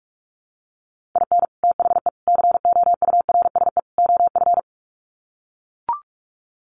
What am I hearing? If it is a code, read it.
Morse code, heard as a keyed tone, about 44 wpm, 700 Hz. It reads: IN THE COURSE OF
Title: Sets of 4 Words 45wpm